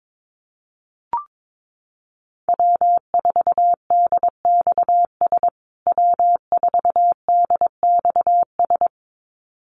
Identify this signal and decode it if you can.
Morse code, heard as a keyed tone, about 22 wpm, 700 Hz. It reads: W4DXS W4DXS
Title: USA Call Signs Repeated 2x 22wpm